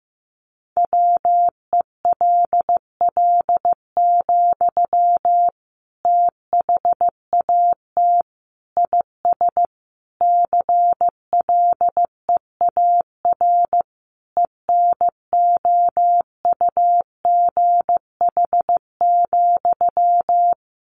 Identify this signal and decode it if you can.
Morse code, heard as a keyed tone, about 15 wpm, 700 Hz. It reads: WELL, THAT IS CLEAR ENOUGH,
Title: Sentences from Top 900 Words 15wpm